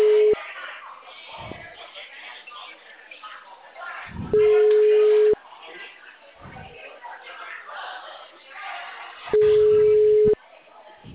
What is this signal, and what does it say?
Ringback tone (Europe, 425 Hz), with speech.